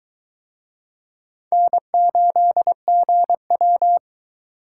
Morse code, heard as a keyed tone, about 22 wpm, 700 Hz. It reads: N8GW